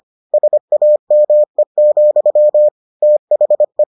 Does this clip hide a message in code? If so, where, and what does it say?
Yes: Morse code from 0.3 to 3.8 s, heard as a keyed tone, about 25 wpm, 600 Hz. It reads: SAME, THE